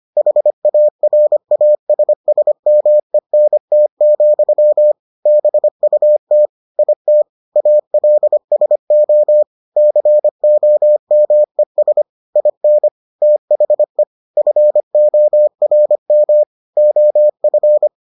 Morse code, heard as a keyed tone, about 25 wpm, 600 Hz. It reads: HARASSMENT, BUT IT ALSO COMES IN THE FORM OF